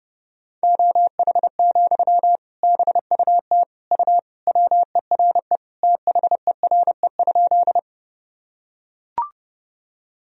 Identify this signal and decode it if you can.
Morse code, heard as a keyed tone, about 30 wpm, 700 Hz. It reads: OH, BUT U WERE THERE?
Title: Sentences from Top 500 Words and Abbreviations 30wpm